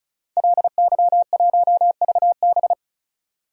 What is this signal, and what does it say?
Morse code, heard as a keyed tone, about 36 wpm, 700 Hz. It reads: LY1VB
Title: Code Groups - 5 Letter:Numbers - 35wpm